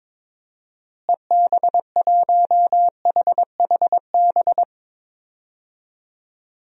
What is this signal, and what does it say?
Morse code, heard as a keyed tone, about 22 wpm, 700 Hz. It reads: EB1HHB
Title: DXCC-100 Call Signs - 6 Characters - Repeated 2x - 22wpm